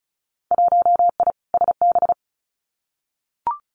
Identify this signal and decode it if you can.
Morse code, heard as a keyed tone, about 34 wpm, 700 Hz. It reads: 1I SB
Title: Winter Field Day Exchanges 35wpm